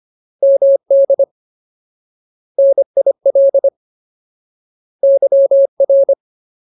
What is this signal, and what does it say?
Morse code, heard as a keyed tone, about 25 wpm, 550 Hz. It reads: MD NIL YR
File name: Ultimate Word List / 25WPM Ultimate